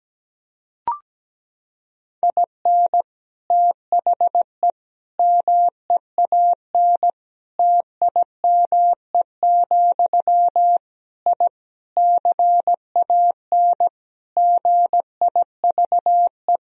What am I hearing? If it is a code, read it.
Morse code, heard as a keyed tone, about 17 wpm, 700 Hz. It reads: IN THE MEAN TIME, I CAN GIVE
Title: Sentences from Top 1000 Words - Review Part 3 - 17wpm